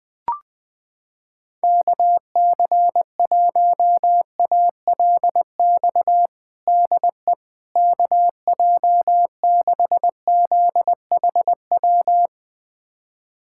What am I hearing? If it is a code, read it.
Morse code, heard as a keyed tone, about 21 wpm, 700 Hz. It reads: KC1ALX DE KJ6ZHW
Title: USA Callsign DE Callsign - 6 Characters - 20wpm